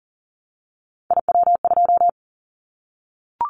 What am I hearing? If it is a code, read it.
Morse code, heard as a keyed tone, about 39 wpm, 700 Hz. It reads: IW2
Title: Callsign Prefixes for Top 50 DXCC - 40wpm